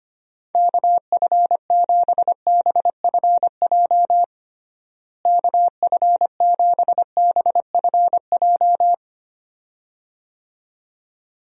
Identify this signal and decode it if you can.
Morse code, heard as a keyed tone, about 25 wpm, 700 Hz. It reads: KF7BFJ KF7BFJ
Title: USA Call Signs - 6 Characters - Repeated 2x - 25wpm